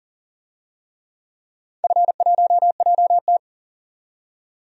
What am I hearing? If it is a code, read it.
Morse code, heard as a keyed tone, about 40 wpm, 700 Hz. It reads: F1JT
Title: DXCC-100 Call Signs - 4 Characters - 40wpm